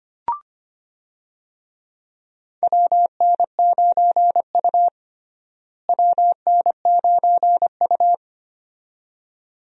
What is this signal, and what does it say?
Morse code, heard as a keyed tone, about 25 wpm, 700 Hz. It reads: WN9U WN9U
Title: USA Call Signs - 4 Characters - Repeated 2x - 25wpm